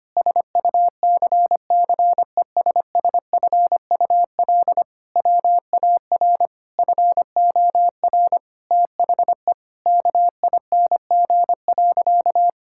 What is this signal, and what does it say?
Morse code, heard as a keyed tone, about 25 wpm, 700 Hz. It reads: SUCCESSFUL WAR FOR THE KING.
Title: Sentences from Top 1000 Words - Review Part 6 - 25wpm